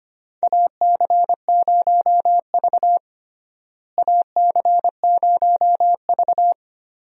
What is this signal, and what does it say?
Morse code, heard as a keyed tone, about 25 wpm, 700 Hz. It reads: AC0V AC0V
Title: USA Call Signs - 4 Characters - Repeated 3x - 25wpm